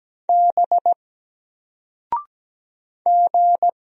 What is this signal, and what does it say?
Morse code, heard as a keyed tone, about 17 wpm, 700 Hz. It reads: B G
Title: Single Letter-Number 17wpm